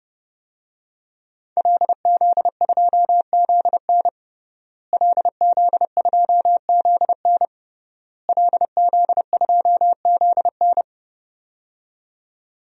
Morse code, heard as a keyed tone, about 30 wpm, 700 Hz. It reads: LZ2ZN LZ2ZN LZ2ZN
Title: DXCC-100 Call Signs - 5 Characters - Repeated 3x - 30wpm